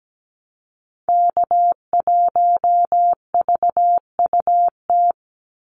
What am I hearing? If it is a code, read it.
Morse code, heard as a keyed tone, about 17 wpm, 700 Hz. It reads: K1VUT
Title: Call Signs - 17wpm